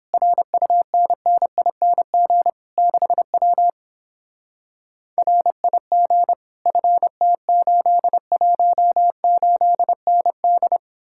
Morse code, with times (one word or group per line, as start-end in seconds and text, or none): 0.1-2.5 RUNNING
2.8-3.7 6W
5.2-6.3 RIG
6.7-10.8 FT818ND